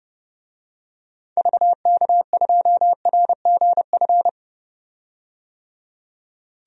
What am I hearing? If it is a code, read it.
Morse code, heard as a keyed tone, about 30 wpm, 700 Hz. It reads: VK2RGF
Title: DXCC-100 Call Signs - 6 Characters - 30wpm